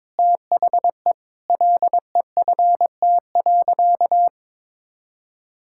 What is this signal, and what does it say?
Morse code, heard as a keyed tone, about 22 wpm, 700 Hz. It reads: THE LEFT.
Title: Sentences from Top 1600 Words 22wpm